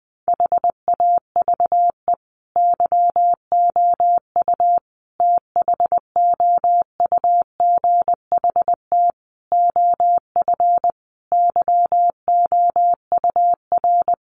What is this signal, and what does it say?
Morse code, heard as a keyed tone, about 19 wpm, 700 Hz. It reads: HAVE YOU THOUGHT OF YOUR
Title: Sentences from Top 700 Words 20wpm